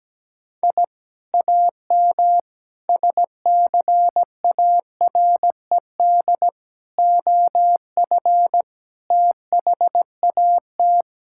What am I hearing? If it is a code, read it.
Morse code, heard as a keyed tone, about 17 wpm, 700 Hz. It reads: I AM SCARED OF THAT
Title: Sentences from Top 1000 Words - Review Part 5 - 17wpm